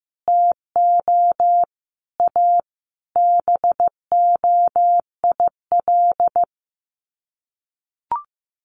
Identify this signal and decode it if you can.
Morse code, heard as a keyed tone, about 15 wpm, 700 Hz. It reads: TO A BOIL
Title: Sets of 5 Words Encore 15wpm